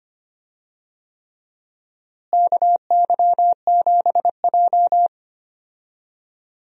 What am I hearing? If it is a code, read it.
Morse code, heard as a keyed tone, about 25 wpm, 700 Hz. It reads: KY7J
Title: USA Call Signs - 4 Characters - Repeated 3x - 25wpm